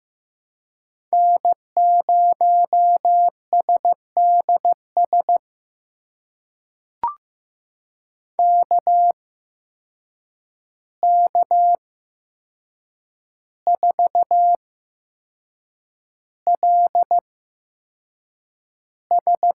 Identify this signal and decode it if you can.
Morse code, heard as a keyed tone, about 15 wpm, spaced as for 6 wpm, 700 Hz. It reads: N0SDS KK4LS
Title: USA Call Signs - 5 Characters - 8x Character Spacing - 15wpm